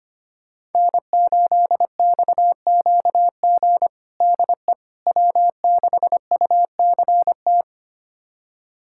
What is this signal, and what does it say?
Morse code, heard as a keyed tone, about 25 wpm, 700 Hz. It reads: N8XQG DE W6UCT
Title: USA Callsign DE Callsign - 5 Characters - 25wpm